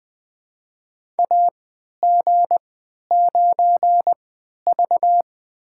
Morse code, heard as a keyed tone, about 20 wpm, spaced as for 11 wpm, 700 Hz. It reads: AG9V
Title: USA Call Signs - 4 Characters - 3x Character Spacing - 20wpm